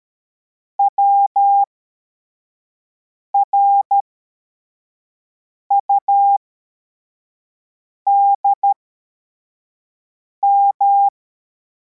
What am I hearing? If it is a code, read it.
Morse code, heard as a keyed tone, about 13 wpm, spaced as for 4 wpm, 800 Hz. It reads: WRUDM